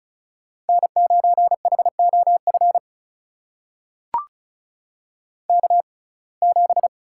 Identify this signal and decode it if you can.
Morse code, heard as a keyed tone, about 36 wpm, spaced as for 26 wpm, 700 Hz. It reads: N9HOF K7
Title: USA Call Signs - 6x Character Spacing - 35wpm